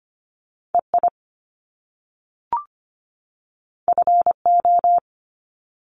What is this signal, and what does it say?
Morse code, heard as a keyed tone, about 25 wpm, 700 Hz. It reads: EI FO
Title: Top Two Letter Combinations in Words 25wpm